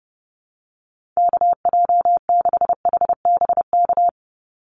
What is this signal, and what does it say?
Morse code, heard as a keyed tone, about 30 wpm, 700 Hz. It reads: KJ6HBK
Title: USA Call Signs 30wpm